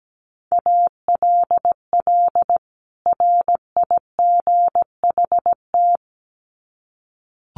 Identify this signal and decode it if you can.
Morse code, heard as a keyed tone, about 17 wpm, 700 Hz. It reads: ALL RIGHT
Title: Sets of 2 Words 17wpm